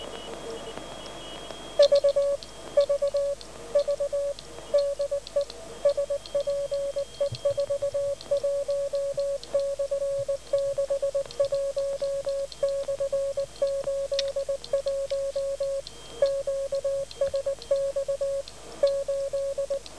Morse code, heard as a keyed tone, about 19 wpm, 550 Hz. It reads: V V V DE SPE41/61/81 QSX 8